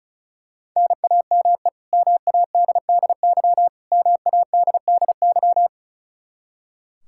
Morse code, heard as a keyed tone, about 36 wpm, 700 Hz. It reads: NAME MADDY MADDY
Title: Practice QSO - Exchange 1 - Beginning - 35wpm